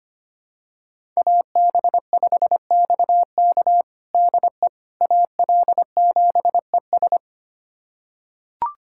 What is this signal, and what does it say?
Morse code, heard as a keyed tone, about 25 wpm, 700 Hz. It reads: AB5XK DE AL7ES
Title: USA Callsign DE Callsign - 5 Characters - 25wpm